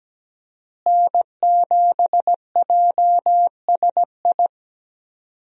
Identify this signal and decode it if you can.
Morse code, heard as a keyed tone, about 17 wpm, 700 Hz. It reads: N7JSI
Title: USA Call Signs - 5 Characters - Repeated 2x - 17wpm